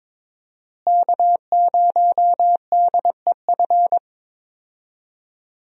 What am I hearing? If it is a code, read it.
Morse code, heard as a keyed tone, about 22 wpm, 700 Hz. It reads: K0DEF
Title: USA Call Signs - 5 Characters - Repeated 2x - 22wpm